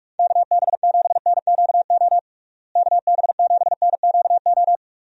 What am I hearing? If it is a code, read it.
Morse code, heard as a keyed tone, about 46 wpm, 700 Hz. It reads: KB7NQO KB7NQO
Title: USA Call Signs - 6 Characters - Repeated 3x - 45wpm